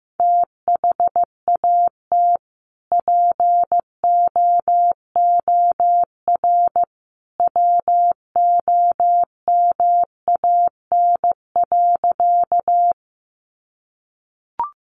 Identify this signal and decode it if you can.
Morse code, heard as a keyed tone, about 15 wpm, 700 Hz. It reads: THAT POOR WOMAN.
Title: Sentences from Top 1100 Words 15wpm